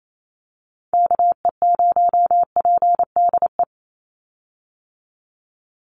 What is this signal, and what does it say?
Morse code, heard as a keyed tone, about 28 wpm, 700 Hz. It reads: KE0PDE
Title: USA Call Signs 28wpm